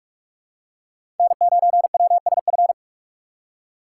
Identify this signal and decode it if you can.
Morse code, heard as a keyed tone, about 46 wpm, 700 Hz. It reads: N9WSF
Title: USA Call Signs - 5 Characters - 2x Character Spacing - 45wpm